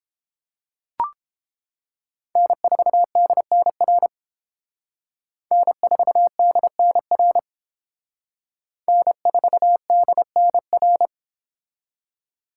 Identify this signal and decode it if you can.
Morse code, heard as a keyed tone, about 30 wpm, 700 Hz. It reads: N4DNR N4DNR N4DNR